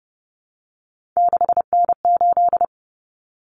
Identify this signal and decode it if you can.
Morse code, heard as a keyed tone, about 30 wpm, 700 Hz. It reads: 6N8
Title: Common Cut Numbers - 3-Digit 30wpm